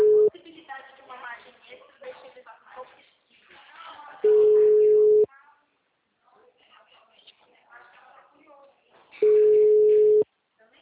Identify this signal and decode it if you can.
Ringback tone (Europe, 425 Hz), with speech.
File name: out- -2007- - - .53744